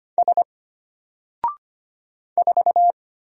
Morse code, heard as a keyed tone, about 25 wpm, 700 Hz. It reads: S 4
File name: Single-Letter-Number-25wpm